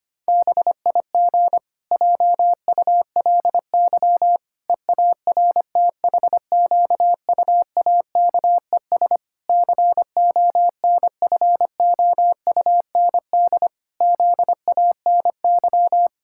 Morse code, heard as a keyed tone, about 25 wpm, 700 Hz. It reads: BIG JULY EARTHQUAKES CONFOUND ZANY
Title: Pangram Challenge v2 - 25wpm